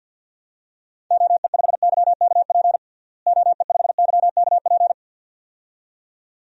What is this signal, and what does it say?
Morse code, heard as a keyed tone, about 52 wpm, 700 Hz. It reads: OE5YKP OE5YKP
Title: DXCC-100 Call Signs - 6 Characters - Repeated 2x - 50wpm